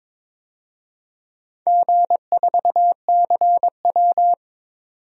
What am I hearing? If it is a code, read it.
Morse code, heard as a keyed tone, about 23 wpm, 700 Hz. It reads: G4CW
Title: DXCC-100 Call Signs - 4 Characters - 22wpm